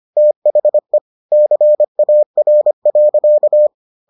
Morse code, heard as a keyed tone, about 25 wpm, 600 Hz. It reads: THE CAR.